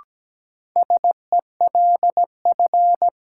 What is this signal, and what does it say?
Morse code, heard as a keyed tone, about 17 wpm, 700 Hz. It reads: SELF
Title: Sentences from Top 1400 Words - New Word Drill Alt - 17wpm